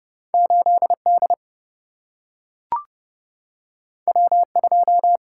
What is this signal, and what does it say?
Morse code, heard as a keyed tone, about 30 wpm, 700 Hz. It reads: 8 D W 2
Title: Single-Letter Single-Number 30wpm